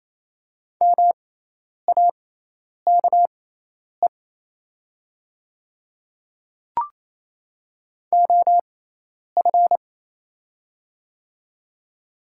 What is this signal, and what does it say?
Morse code, heard as a keyed tone, about 28 wpm, spaced as for 10 wpm, 700 Hz. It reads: MAKE OF
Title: 6x Character Spacing - Top 100 Words - 28wpm